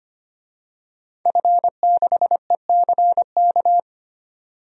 Morse code, heard as a keyed tone, about 25 wpm, 700 Hz. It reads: F6ECK
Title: DXCC-100 Call Signs - 5 Characters - 25wpm